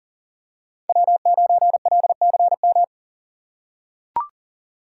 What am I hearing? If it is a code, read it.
Morse code, heard as a keyed tone, about 40 wpm, 700 Hz. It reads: W9LCM
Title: USA Call Signs - 5 Characters - 3x Character Spacing - 40wpm